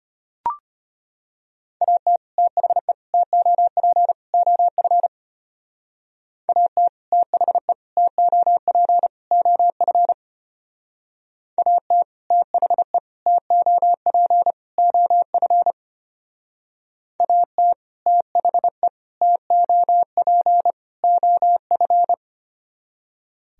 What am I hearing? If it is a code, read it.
Morse code, heard as a keyed tone, about 32 wpm, 700 Hz. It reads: AT THE TOP OF AT THE TOP OF AT THE TOP OF AT THE TOP OF